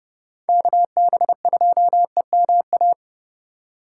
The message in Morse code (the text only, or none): KB2EMA